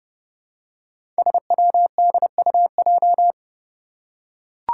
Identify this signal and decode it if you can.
Morse code, heard as a keyed tone, about 32 wpm, 700 Hz. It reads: SWDUJ
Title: Five Challenging Letter-Number Combinations 30wpm